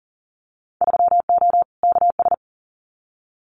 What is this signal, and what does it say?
Morse code, heard as a keyed tone, about 40 wpm, 700 Hz. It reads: 3O KS
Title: Winter Field Day Exchanges 40wpm